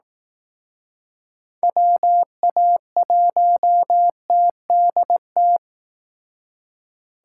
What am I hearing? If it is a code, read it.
Morse code, heard as a keyed tone, about 18 wpm, 700 Hz. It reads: WA1TDT
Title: Speed Racing - USA 2x3 Call Signs 15wpm